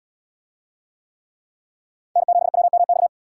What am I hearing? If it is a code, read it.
Morse code, heard as a keyed tone, about 68 wpm, 700 Hz. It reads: N7WMX